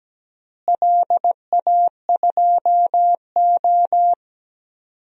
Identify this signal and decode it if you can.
Morse code, heard as a keyed tone, about 17 wpm, 700 Hz. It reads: LA2O
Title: DXCC-100 Call Signs - 4 Characters - Repeated 3x - 17wpm